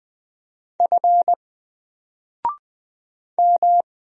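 Morse code, heard as a keyed tone, about 20 wpm, 700 Hz. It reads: F M
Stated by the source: Single letter and number Morse drill at 20 wpm, from N4LCD.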